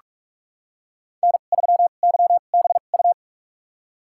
Morse code, heard as a keyed tone, about 46 wpm, spaced as for 32 wpm, 700 Hz. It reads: N3YBU
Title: USA Call Signs - 5 Characters - 2x Character Spacing - 45wpm